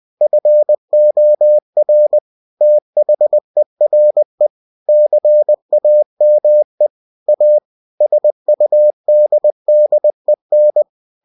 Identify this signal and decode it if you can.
Morse code, heard as a keyed tone, about 20 wpm, 600 Hz. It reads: FOR THERE CAME A SUDDEN
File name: cw-books / winnie-the-pooh / winnie-20wpm-page-0085